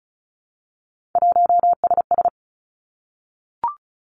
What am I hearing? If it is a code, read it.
Morse code, heard as a keyed tone, about 36 wpm, 700 Hz. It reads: 1SS
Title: Three Challenging Letter-Number Combinations 35wpm